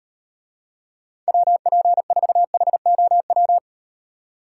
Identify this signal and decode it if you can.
Morse code, heard as a keyed tone, about 38 wpm, 700 Hz. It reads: WP4HOW